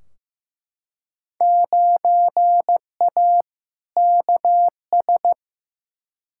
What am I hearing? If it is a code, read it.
Morse code, heard as a keyed tone, about 15 wpm, 700 Hz. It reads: 9A KS
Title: Field Day Exchanges Encore - 15wpm